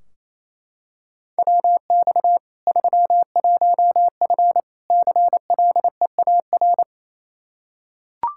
Morse code, heard as a keyed tone, about 28 wpm, 700 Hz. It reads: WX 31F CLEAR
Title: Practice QSO - Exchange 2 - Info - 28wpm